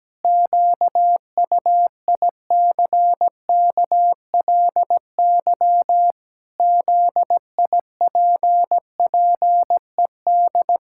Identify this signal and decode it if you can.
Morse code, heard as a keyed tone, about 17 wpm, 700 Hz. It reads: QUICKLY ZIPPED
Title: Pangram Challenge v2 - 17wpm